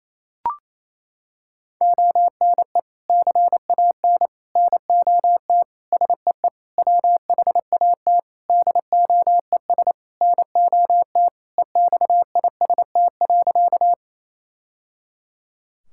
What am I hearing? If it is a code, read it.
Morse code, heard as a keyed tone, about 28 wpm, 700 Hz. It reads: ONE CAN NOT SEE WHAT DOES NOT EXIST.
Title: Sentences from Top 1000 Words - Review Part 5 - 28wpm